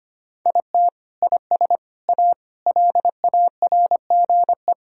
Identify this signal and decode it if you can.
Morse code, heard as a keyed tone, about 25 wpm, 700 Hz. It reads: IT IS A LARGE
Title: Sentences from Top 1000 Words - Review Part 1 - 25wpm